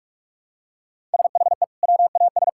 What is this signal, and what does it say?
Morse code, heard as a keyed tone, about 46 wpm, 700 Hz. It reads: SHE WAS
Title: Sentences from Top 1000 Words - Review Part 6 - 45wpm